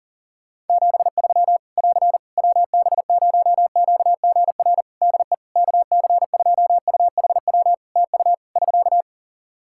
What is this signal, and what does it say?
Morse code, heard as a keyed tone, about 40 wpm, 700 Hz. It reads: 73 <AR> WB0QGR DE KC2UHW TU <SK>
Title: Practice QSO - Exchange 4 - Ending - 40wpm